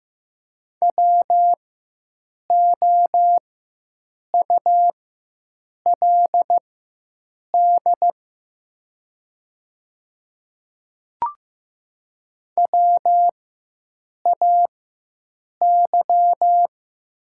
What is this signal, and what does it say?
Morse code, heard as a keyed tone, about 15 wpm, spaced as for 7 wpm, 700 Hz. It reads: WOULD WAY
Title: 4x Character Spacing - Top 100 Words - 15wpm